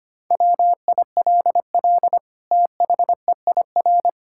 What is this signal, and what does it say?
Morse code, heard as a keyed tone, about 25 wpm, 700 Hz. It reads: WILL THEIR
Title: Sentences from Top 1000 Words - Review Part 5 - 25wpm